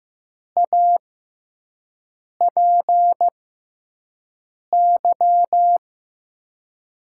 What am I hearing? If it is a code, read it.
Morse code, heard as a keyed tone, about 15 wpm, spaced as for 5 wpm, 700 Hz. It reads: APY
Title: USA Call Signs - 6x Character Spacing - 15wpm